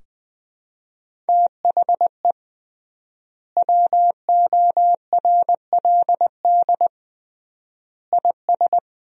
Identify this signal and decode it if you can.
Morse code, heard as a keyed tone, about 20 wpm, 700 Hz. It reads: THE WORLD IS
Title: 3x Word-Spacing - Sentences from Top 200 Words - 20wpm